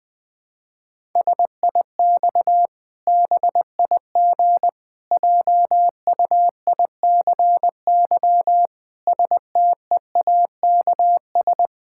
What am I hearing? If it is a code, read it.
Morse code, heard as a keyed tone, about 21 wpm, 700 Hz. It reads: SIX BIG JUICY STEAKS
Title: Pangram Challenge v2 - 20wpm